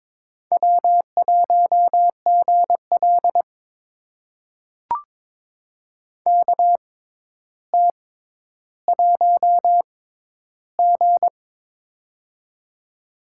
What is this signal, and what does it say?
Morse code, heard as a keyed tone, about 23 wpm, spaced as for 11 wpm, 700 Hz. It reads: W1GL KT1G
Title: USA Call Signs - 4 Characters - 6x Character Spacing - 22wpm